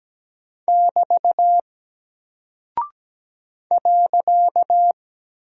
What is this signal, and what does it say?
Morse code, heard as a keyed tone, about 17 wpm, 700 Hz. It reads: <BT> .
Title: Single Character (Letter:Number:Punctuation:Prosign) Alt 17wpm